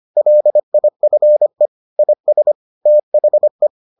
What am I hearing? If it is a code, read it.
Morse code, heard as a keyed tone, about 25 wpm, 600 Hz. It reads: LIFE IS THE